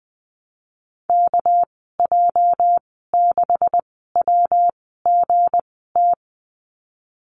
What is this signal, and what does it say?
Morse code, heard as a keyed tone, about 20 wpm, spaced as for 15 wpm, 700 Hz. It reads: KJ6WGT
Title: 207 Lesson 33g - Letter j - Call Signs 20wpm